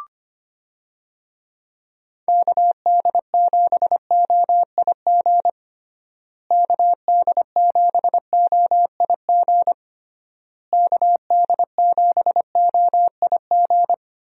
Morse code, heard as a keyed tone, about 25 wpm, 700 Hz. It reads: KD7OIG KD7OIG KD7OIG
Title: USA Call Signs - 6 Characters - Repeated 3x - 25wpm